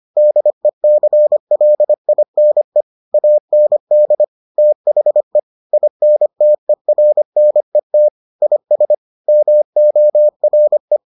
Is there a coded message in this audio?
Yes: Morse code from 0.2 to 11.0 s, heard as a keyed tone, about 25 wpm, 600 Hz. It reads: DECLINE AND THE INTERNET IS MORE